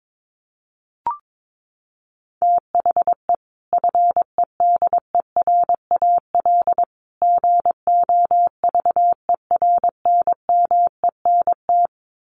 Morse code, heard as a keyed tone, about 22 wpm, 700 Hz. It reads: THE FEDERAL GOVERNMENT
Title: Sets of 3 Words 22wpm